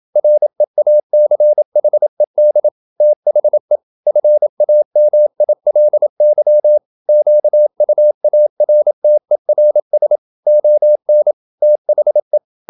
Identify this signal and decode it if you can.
Morse code, heard as a keyed tone, about 27 wpm, 600 Hz. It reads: REACHED THE FAMILY QUARTERS ON THE
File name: Never 0027wpm